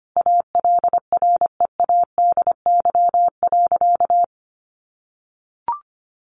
Morse code, heard as a keyed tone, about 25 wpm, 700 Hz. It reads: ALREADY.
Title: Sentences from Top 1000 Words 25wpm